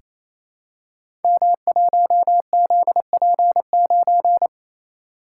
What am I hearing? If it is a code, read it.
Morse code, heard as a keyed tone, about 27 wpm, 700 Hz. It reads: M1ZP9
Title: Code Groups - 5 Letter:Numbers - 28wpm